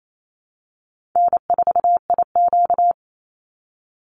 Morse code, heard as a keyed tone, about 28 wpm, 700 Hz. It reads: N4IQ
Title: Call Signs - 28wpm